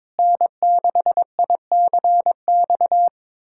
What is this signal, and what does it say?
Morse code, heard as a keyed tone, about 21 wpm, 700 Hz. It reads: N6ICX
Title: USA Callsign DE Callsign - 5 Characters - 22wpm